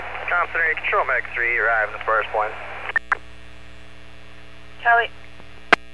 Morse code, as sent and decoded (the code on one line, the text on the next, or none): none